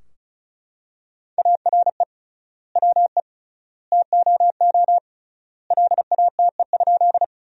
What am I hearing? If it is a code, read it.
Morse code, heard as a keyed tone, about 34 wpm, 700 Hz. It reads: ARE WE TOO LATE?
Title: 3x Word-Spacing - Sentences from Top 300 Words - 35wpm